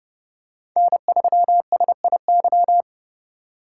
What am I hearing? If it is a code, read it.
Morse code, heard as a keyed tone, about 30 wpm, 700 Hz. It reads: N3SIY